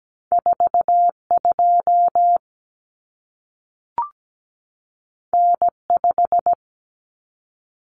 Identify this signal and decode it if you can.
Morse code, heard as a keyed tone, about 17 wpm, 700 Hz. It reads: 42 N5
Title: Common Cut Numbers - 2-Digit 17wpm